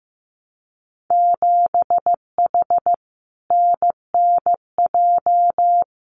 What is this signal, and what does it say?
Morse code, heard as a keyed tone, about 15 wpm, 700 Hz. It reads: 7H NNJ
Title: Winter Field Day Exchanges Encore 15wpm